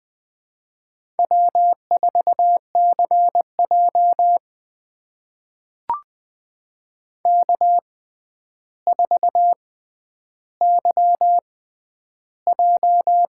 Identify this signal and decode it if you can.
Morse code, heard as a keyed tone, about 21 wpm, spaced as for 10 wpm, 700 Hz. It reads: W4CJ K4YJ
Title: USA Call Signs - 4 Characters - 6x Character Spacing - 20wpm